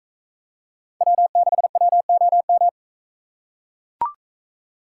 Morse code, heard as a keyed tone, about 41 wpm, 700 Hz. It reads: W6WOM